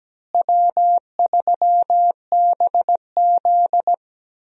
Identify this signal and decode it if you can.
Morse code, heard as a keyed tone, about 17 wpm, 700 Hz. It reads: W3BZ
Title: USA Call Signs - 4 Characters - Repeated 3x - 17wpm